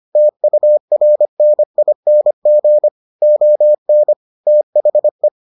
Morse code, heard as a keyed tone, about 25 wpm, 600 Hz. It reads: TURNING ON THE